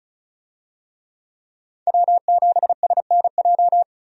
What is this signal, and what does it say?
Morse code, heard as a keyed tone, about 34 wpm, 700 Hz. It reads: W7SNJ
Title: USA Call Signs - 5 Characters - 35wpm